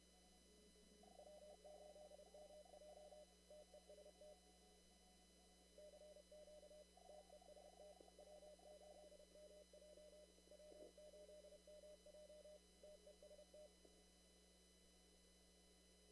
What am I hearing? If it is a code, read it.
Morse code, heard as a keyed tone, about 30 wpm, 600 Hz, with noise. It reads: W8MJ TEST CQ TEST W8MJ W8MJ TEST
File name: CW / kt5j wpx cw-op-k5pi